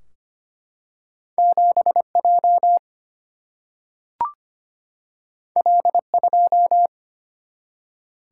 Morse code, heard as a keyed tone, about 25 wpm, 700 Hz. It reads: 7 J L 2
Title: Single-Letter Single-Number 25wpm